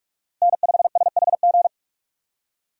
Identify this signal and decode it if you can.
Morse code, heard as a keyed tone, about 45 wpm, 700 Hz. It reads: N5SHG
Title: USA Call Signs - 5 Characters - 3x Character Spacing - 45wpm